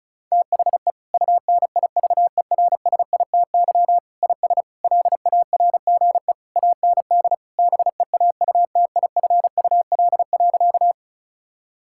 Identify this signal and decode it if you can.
Morse code, heard as a keyed tone, about 36 wpm, 690 Hz. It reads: THE UNIVERSITY IS LARGE AND BEAUTIFUL.
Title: Sentences from Top 1000 Words - Review Part 5 - 35wpm